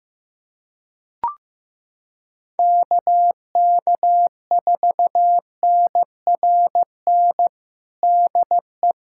Morse code, heard as a keyed tone, about 15 wpm, 700 Hz. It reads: KK4NRN DE
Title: USA Callsign DE Callsign - 6 Characters - 15wpm